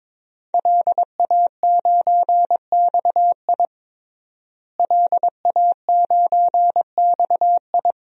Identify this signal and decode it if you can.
Morse code, heard as a keyed tone, about 22 wpm, 700 Hz. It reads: LA9XI LA9XI
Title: DXCC-100 Call Signs - 5 Characters - Repeated 3x - 22wpm